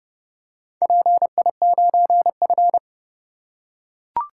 Morse code, heard as a keyed tone, about 28 wpm, 700 Hz. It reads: PI9F